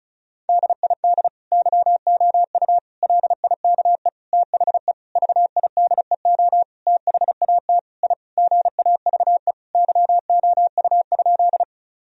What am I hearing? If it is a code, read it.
Morse code, heard as a keyed tone, about 34 wpm, 700 Hz. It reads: DID YOU LIKE THE VIDEO THAT I GAVE YOU?
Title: Sentences from Top 1000 Words - Review Part 4 - 35wpm